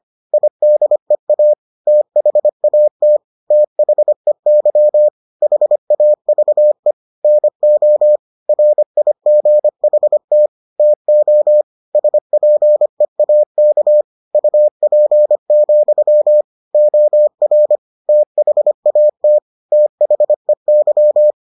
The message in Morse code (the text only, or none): IDEA THAT THEY HAVE NO RIGHT TO SPEAK UP, OR THAT THEY